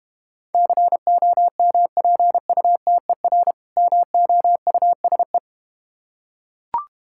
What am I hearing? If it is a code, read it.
Morse code, heard as a keyed tone, about 32 wpm, 700 Hz. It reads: COMPUTER MOUSE